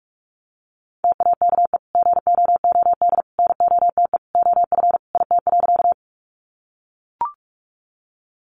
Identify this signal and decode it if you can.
Morse code, heard as a keyed tone, about 44 wpm, 700 Hz. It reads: TAKE GOOD NOTE OF IT.
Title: Sentences from Top 500 Words 45wpm